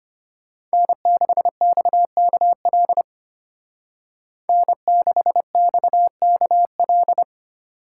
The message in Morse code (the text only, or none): N6XKL N6XKL